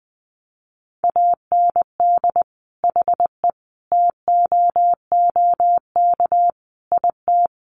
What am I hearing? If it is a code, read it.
Morse code, heard as a keyed tone, about 20 wpm, 700 Hz. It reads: AND HE TOOK IT
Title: Sentences from Top 1400 Words 20wpm